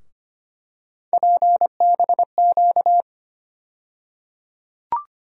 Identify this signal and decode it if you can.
Morse code, heard as a keyed tone, about 25 wpm, 700 Hz. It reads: PBQ
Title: USA Call Sign Postfixes - 3 Letters 25wpm